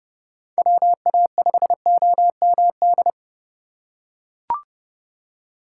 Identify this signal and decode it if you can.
Morse code, heard as a keyed tone, about 30 wpm, 700 Hz. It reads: WA5OMD
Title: USA Call Signs - 6 Characters - 3x Character Spacing - 30wpm